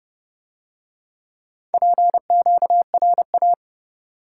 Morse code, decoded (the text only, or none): PQRA